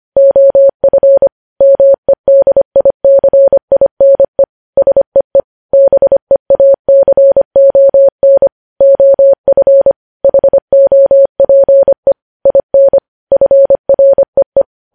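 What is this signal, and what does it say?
Morse code, heard as a keyed tone, about 25 wpm, 550 Hz. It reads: OF MEDICINE SEE BEACON OF HOPE IN FREE
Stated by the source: Morse code news headlines sent at 25 words per minute.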